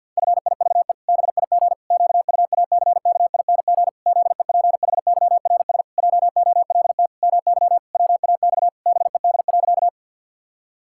Morse code, heard as a keyed tone, about 51 wpm, 700 Hz. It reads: FIVE BIG QUACKING ZEPHYRS JOLT MY WAX BED.
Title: Pangram Challenge v2 - 50wpm